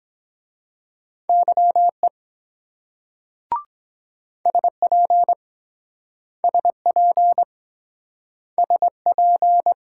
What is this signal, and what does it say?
Morse code, heard as a keyed tone, about 23 wpm, 700 Hz. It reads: YE SP SP SP